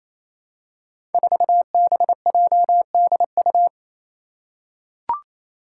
Morse code, heard as a keyed tone, about 28 wpm, 700 Hz. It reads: 4BJDU
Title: Five Challenging Letter-Number Combinations 28wpm